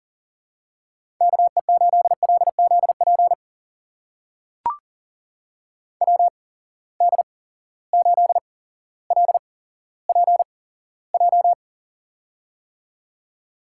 Morse code, heard as a keyed tone, about 37 wpm, spaced as for 17 wpm, 700 Hz. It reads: KE8LZP WD8LPJ
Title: USA Call Signs - 6 Characters - 8x Character Spacing - 40wpm